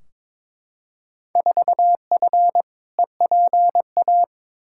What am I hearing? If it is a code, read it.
Morse code, heard as a keyed tone, about 22 wpm, 700 Hz. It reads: 4F EPA
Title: Field Day Exchanges Encore - 22wpm